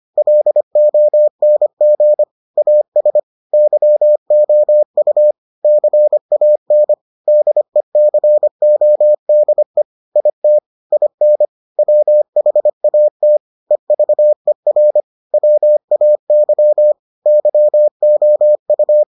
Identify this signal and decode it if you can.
Morse code, heard as a keyed tone, about 25 wpm, 600 Hz. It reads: LONG AS YOU CAN DECODE IT IN WHAT EVER WAY YOU